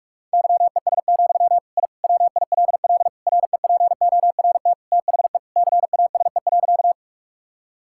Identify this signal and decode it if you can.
Morse code, heard as a keyed tone, about 46 wpm, 700 Hz. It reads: YES, I WILL REPORT THE CASE.
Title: Sentences from Top 1000 Words - Review Part 6 - 45wpm